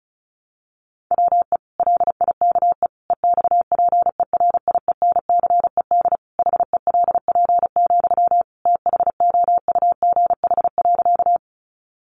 Morse code, heard as a keyed tone, about 34 wpm, 700 Hz. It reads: WE LIKE EXPERIENCED HELP, THOUGH.
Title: Sentences from Top 1100 Words 35wpm